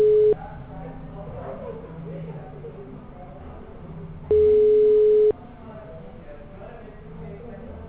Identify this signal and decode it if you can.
Ringback tone (Europe, 425 Hz), with speech.
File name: out- -2004- - - .28150